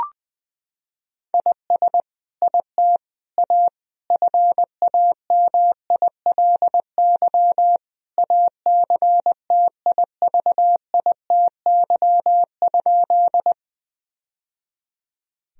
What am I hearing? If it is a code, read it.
Morse code, heard as a keyed tone, about 21 wpm, 700 Hz. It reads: IS IT A FAMILY ACTIVITY?
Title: Sentences from Top 1000 Words - Review Part 4 - 20wpm